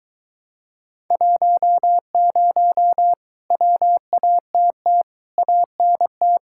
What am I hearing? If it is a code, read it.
Morse code, heard as a keyed tone, about 23 wpm, 700 Hz. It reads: 10 WATT ANT